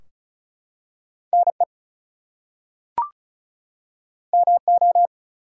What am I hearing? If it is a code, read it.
Morse code, heard as a keyed tone, about 36 wpm, 700 Hz. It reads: NE MO
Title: US States and Canadian Provinces 35wpm